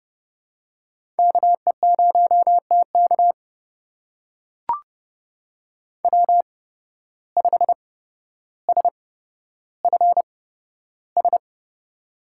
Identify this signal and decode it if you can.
Morse code, heard as a keyed tone, about 32 wpm, spaced as for 13 wpm, 700 Hz. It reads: KE0TK W5SFS
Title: USA Call Signs - 5 Characters - 8x Character Spacing - 30wpm